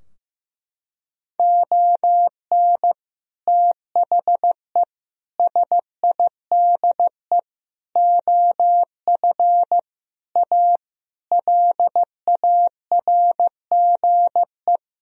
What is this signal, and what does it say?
Morse code, heard as a keyed tone, about 15 wpm, 700 Hz. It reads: ON THE SIDE OF A LARGE
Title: Sentences from Top 1000 Words - Review Part 3 - 15wpm